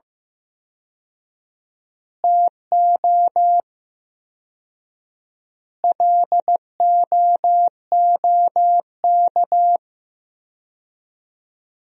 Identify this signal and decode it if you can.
Morse code, heard as a keyed tone, about 15 wpm, 700 Hz. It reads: TO LOOK
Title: 4x Word-Spacing - Sentences from Top 200 Words - 15wpm